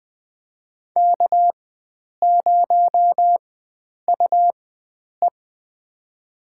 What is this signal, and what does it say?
Morse code, heard as a keyed tone, about 20 wpm, spaced as for 9 wpm, 700 Hz. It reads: K0UE
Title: USA Call Signs - 4 Characters - 4x Character Spacing - 20wpm